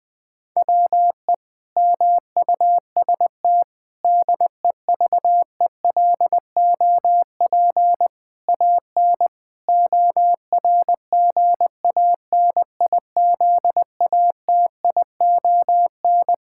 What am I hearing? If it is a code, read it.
Morse code, heard as a keyed tone, about 20 wpm, 700 Hz. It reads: WE MUST DEVELOP AN ORGANIZATION
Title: Sentences from Top 1000 Words - Review Part 5 - 20wpm